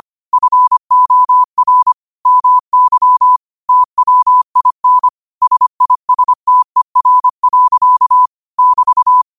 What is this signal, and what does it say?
Morse code, heard as a keyed tone, about 25 wpm, 1000 Hz. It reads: FOR MY TWIN SISTER. <BT>